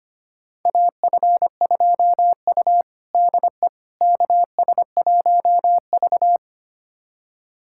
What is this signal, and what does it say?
Morse code, heard as a keyed tone, about 25 wpm, 700 Hz. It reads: AF2U DE KS1V
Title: USA Callsign DE Callsign - 4 Characters - 25wpm